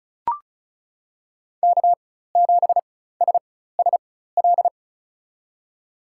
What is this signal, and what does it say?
Morse code, heard as a keyed tone, about 36 wpm, spaced as for 16 wpm, 700 Hz. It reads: K7SSL
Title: USA Call Signs - 4x Character Spacing - 35wpm